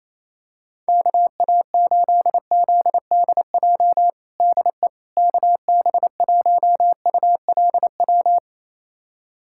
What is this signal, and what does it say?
Morse code, heard as a keyed tone, about 28 wpm, 700 Hz. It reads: KA8ZDJ DE KB1ULW
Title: USA Callsign DE Callsign 28wpm